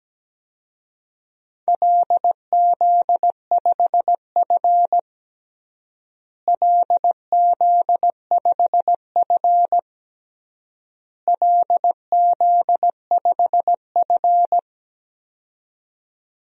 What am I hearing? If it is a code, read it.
Morse code, heard as a keyed tone, about 17 wpm, 700 Hz. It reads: LZ5F LZ5F LZ5F
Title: DXCC-100 Call Signs - 4 Characters - Repeated 3x - 17wpm